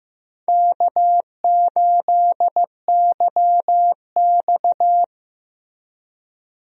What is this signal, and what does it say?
Morse code, heard as a keyed tone, about 15 wpm, 700 Hz. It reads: K8YX
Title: USA Call Signs - 4 Characters - Repeated 2x - 15wpm